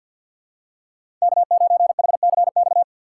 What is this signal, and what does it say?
Morse code, heard as a keyed tone, about 53 wpm, 700 Hz. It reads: K9HCX
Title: USA Call Signs - 5 Characters - 3x Character Spacing - 50wpm